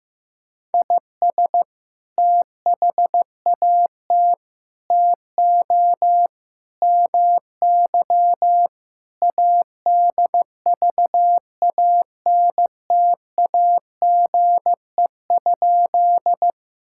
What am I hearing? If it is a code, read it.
Morse code, heard as a keyed tone, about 14 wpm, 700 Hz. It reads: IS THAT TO MY ADVANTAGE?
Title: Sentences from Top 1000 Words - Review Part 6 - 15wpm